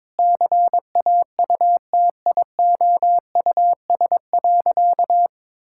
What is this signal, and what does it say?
Morse code, heard as a keyed tone, about 23 wpm, 700 Hz. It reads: CAUTIOUS.
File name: practice / 22wpm / Sentences with 8-Word Phrases - 22wpm